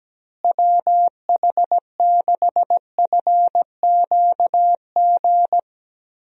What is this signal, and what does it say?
Morse code, heard as a keyed tone, about 17 wpm, 700 Hz. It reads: WH6FQG
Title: USA Call Signs - 6 Characters - Repeated 3x - 17wpm